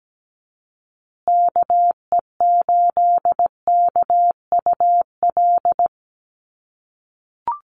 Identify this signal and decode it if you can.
Morse code, heard as a keyed tone, about 17 wpm, 700 Hz. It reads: KE8KUL
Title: USA Call Signs Repeated 3x 17wpm